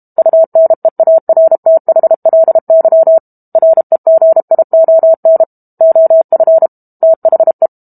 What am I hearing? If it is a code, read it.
Morse code, heard as a keyed tone, about 32 wpm, 650 Hz. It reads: UNEARTHLY REGION OF THE
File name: Chapter 6 - 32 WPM